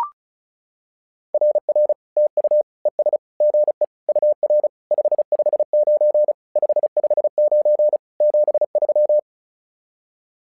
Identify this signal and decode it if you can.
Morse code, heard as a keyed tone, about 34 wpm, 600 Hz. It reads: RR TU ES GE UR 559 559 73
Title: SOTA QSO Practice 35wpm